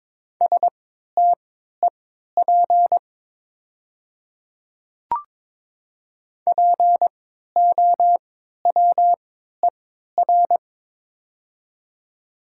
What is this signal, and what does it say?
Morse code, heard as a keyed tone, about 22 wpm, spaced as for 13 wpm, 700 Hz. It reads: STEP POWER
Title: 3x Character Spacing - Top 500 Words - 22wpm